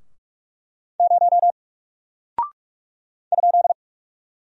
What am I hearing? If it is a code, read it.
Morse code, heard as a keyed tone, about 44 wpm, 700 Hz. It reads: 0 ?
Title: Single Character (Letter:Number:Punctuation:Prosign) 45wpm